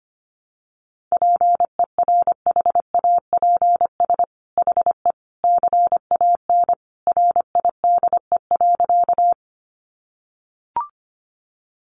Morse code, heard as a keyed tone, about 25 wpm, 700 Hz. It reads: PERHAPS HE CAN RIDE.
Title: Sentences from Top 1200 Words 25wpm